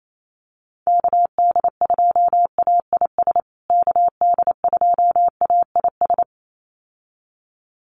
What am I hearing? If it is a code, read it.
Morse code, heard as a keyed tone, about 28 wpm, 700 Hz. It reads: KD2AIS KD2AIS
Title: USA Call Signs Repeated 2x 28wpm